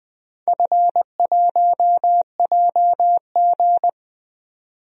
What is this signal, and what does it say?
Morse code, heard as a keyed tone, about 20 wpm, 700 Hz. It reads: F1JG
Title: DXCC-100 Call Signs - 4 Characters - 20wpm